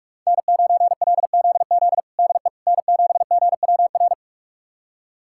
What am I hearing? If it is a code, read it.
Morse code, heard as a keyed tone, about 46 wpm, 700 Hz. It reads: N9LZZ DE N7GWR
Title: USA Callsign DE Callsign - 5 Characters - 45wpm